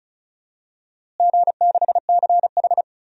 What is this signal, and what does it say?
Morse code, heard as a keyed tone, about 36 wpm, 700 Hz. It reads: G6CH
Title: DXCC-100 Call Signs - 4 Characters - Repeated 2x - 35wpm